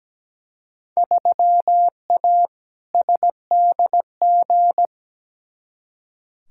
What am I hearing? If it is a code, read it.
Morse code, heard as a keyed tone, about 17 wpm, 700 Hz. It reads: 3A SDG
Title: Field Day Exchanges - 17wpm